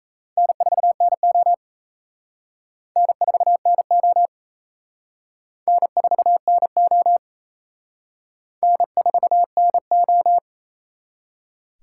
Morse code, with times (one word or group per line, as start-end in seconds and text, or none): 0.4-1.5 N4NO
3.0-4.3 N4NO
5.7-7.2 N4NO
8.6-10.4 N4NO